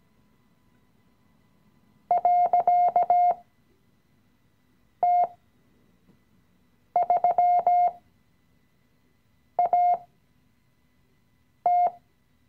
Morse code, heard as a keyed tone, about 17 wpm, spaced as for 5 wpm, 710 Hz. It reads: .T3AT
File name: CW-Online-Kurs / Anfänger / Lektion 017